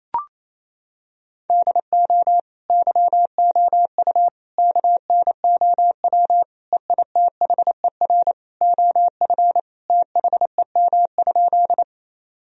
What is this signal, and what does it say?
Morse code, heard as a keyed tone, about 27 wpm, 700 Hz. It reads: DO YOU KNOW EITHER OF THEM?
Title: Sentences from Top 1000 Words - Review Part 4 - 28wpm